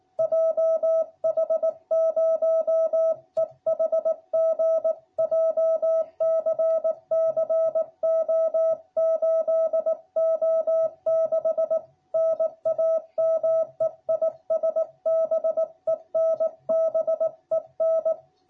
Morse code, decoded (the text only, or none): JH0EHG JCC O8O6 NAME IS BEN BEN